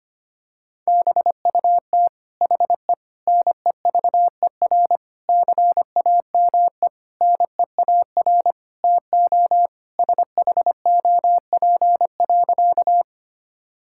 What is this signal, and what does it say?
Morse code, heard as a keyed tone, about 25 wpm, 700 Hz. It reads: BUT HE NEVER CAME NEAR TO SHOP.
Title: Sentences from Top 1600 Words 25wpm